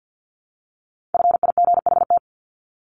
Morse code, heard as a keyed tone, about 49 wpm, 700 Hz. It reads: FIGHT
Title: ING Suffix - Top 100 Words with Base Words 50wpm